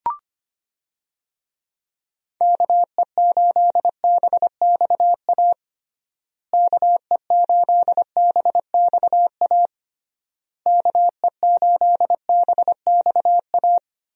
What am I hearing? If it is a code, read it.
Morse code, heard as a keyed tone, about 25 wpm, 700 Hz. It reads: KE8BXA KE8BXA KE8BXA
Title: USA Call Signs - 6 Characters - Repeated 3x - 25wpm